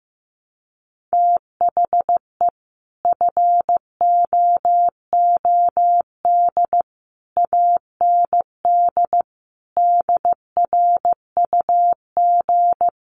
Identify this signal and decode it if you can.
Morse code, heard as a keyed tone, about 15 wpm, 700 Hz. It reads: THE FOOD AND DRUG
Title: Sets of 5 Words 15wpm